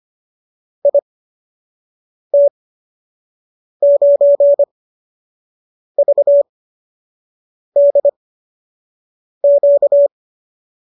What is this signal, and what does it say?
Morse code, heard as a keyed tone, about 25 wpm, spaced as for 6 wpm, 570 Hz. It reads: IT9VDQ